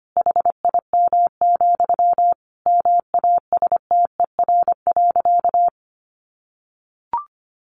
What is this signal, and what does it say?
Morse code, heard as a keyed tone, about 25 wpm, 700 Hz. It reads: HIM, MASTER.
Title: Sentences from Top 1300 Words 25wpm